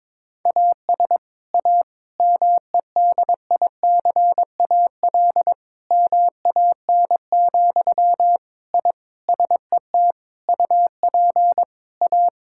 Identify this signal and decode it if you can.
Morse code, heard as a keyed tone, about 22 wpm, 700 Hz. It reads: AS A MEDICAL MAN, I SET UP A
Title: Sentences from Top 1000 Words - Review Part 5 - 22wpm